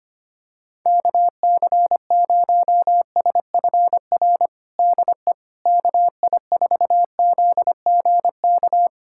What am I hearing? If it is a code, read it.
Morse code, heard as a keyed tone, about 25 wpm, 700 Hz. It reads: KC0SFR DE KI4ZGK
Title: USA Callsign DE Callsign - 6 Characters - 25wpm